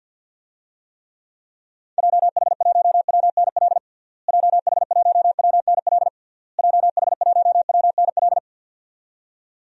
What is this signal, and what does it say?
Morse code, heard as a keyed tone, about 53 wpm, 700 Hz. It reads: JH1WNL JH1WNL JH1WNL
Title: DXCC-100 Call Signs - 6 Characters - Repeated 3x - 50wpm